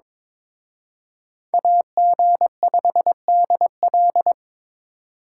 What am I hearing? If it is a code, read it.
Morse code, heard as a keyed tone, about 26 wpm, 700 Hz. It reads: AG5DL
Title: Speed Racing - USA Call Signs 22wpm